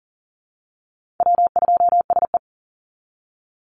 Morse code, heard as a keyed tone, about 37 wpm, 700 Hz. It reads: W2SE
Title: NA SOTA Chaser Callsigns 40wpm